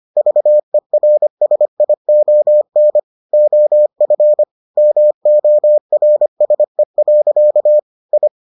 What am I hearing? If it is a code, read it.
Morse code, heard as a keyed tone, about 25 wpm, 600 Hz. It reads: VERSION OF MORSE. I